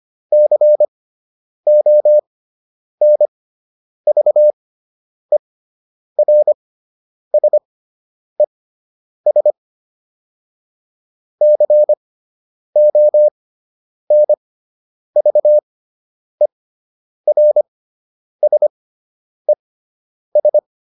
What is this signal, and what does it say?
Morse code, heard as a keyed tone, about 25 wpm, spaced as for 9 wpm, 600 Hz. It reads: CONVERSES CONVERSES